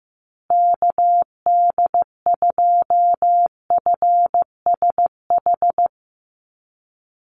Morse code, heard as a keyed tone, about 15 wpm, 700 Hz. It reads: KD2FSH
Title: Call Signs Repeated 2x - 15wpm